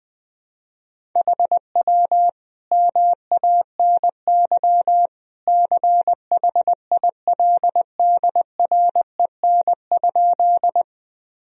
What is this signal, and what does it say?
Morse code, heard as a keyed tone, about 21 wpm, 700 Hz. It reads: HW MANY CHILDREN?
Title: Sentences from Top 300 Words and Abbreviations 20wpm